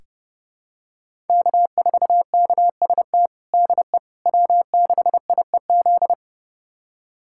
Morse code, heard as a keyed tone, about 30 wpm, 700 Hz. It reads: K4KST DE W6IEZ
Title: USA Callsign DE Callsign - 5 Characters - 30wpm